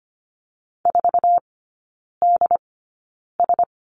Morse code, heard as a keyed tone, about 25 wpm, spaced as for 9 wpm, 700 Hz. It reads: 4DS
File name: course / 25:10wpm / 093 Lesson 16e - Letter c - Three Characters 25wpm